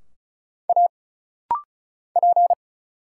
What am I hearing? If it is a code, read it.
Morse code, heard as a keyed tone, about 36 wpm, 700 Hz. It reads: A P
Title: Single Letter-Number - Rapid-Fire 35wpm